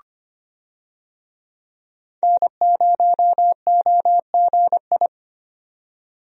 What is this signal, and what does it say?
Morse code, heard as a keyed tone, about 25 wpm, 690 Hz. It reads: N0OGI
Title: USA Call Signs - 5 Characters - 25wpm